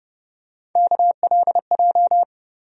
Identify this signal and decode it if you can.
Morse code, heard as a keyed tone, about 30 wpm, 700 Hz. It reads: KLJ
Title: USA Call Sign Postfixes - 3 Letters 30wpm